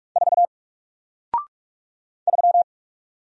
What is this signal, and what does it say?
Morse code, heard as a keyed tone, about 44 wpm, 700 Hz. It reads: 4 3
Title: Single Character (Letter:Number:Punctuation:Prosign) 45wpm